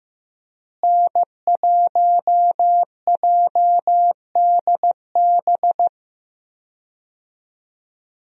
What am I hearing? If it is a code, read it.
Morse code, heard as a keyed tone, about 15 wpm, 700 Hz. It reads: N1JDB
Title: USA Call Signs - 5 Characters - Repeated 2x - 15wpm